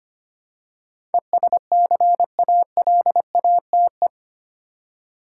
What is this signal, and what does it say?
Morse code, heard as a keyed tone, about 25 wpm, 700 Hz. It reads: ESCALATE
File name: Eight Letter Words - 25wpm